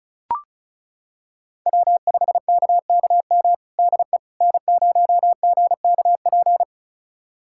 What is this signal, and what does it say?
Morse code, heard as a keyed tone, about 36 wpm, 700 Hz. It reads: W5KKM DE N0GKP
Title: USA Callsign DE Callsign - 5 Characters - 35wpm